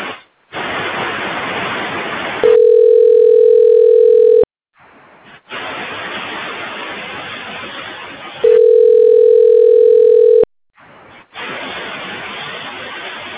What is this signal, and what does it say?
Ringback tone (North America), 440 + 480 Hz.